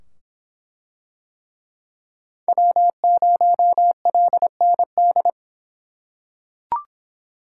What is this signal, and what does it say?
Morse code, heard as a keyed tone, about 24 wpm, 700 Hz. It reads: W0LND